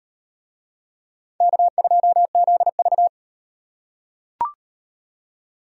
Morse code, heard as a keyed tone, about 37 wpm, 700 Hz. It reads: K2ZV